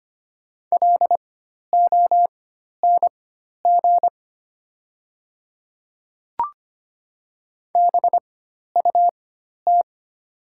Morse code, heard as a keyed tone, about 25 wpm, spaced as for 12 wpm, 700 Hz. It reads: LONG BUT
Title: 4x Character Spacing - Top 500 Words - 25wpm